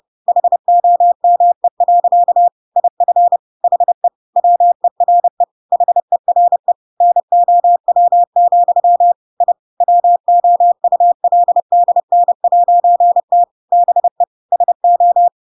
Morse code, heard as a keyed tone, about 30 wpm, 700 Hz. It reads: HOME. IF HE WERE HERE NOW, I WOULDN'T BE SO
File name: Pinocchio / Pinocchio ch5 30 30 ik1hge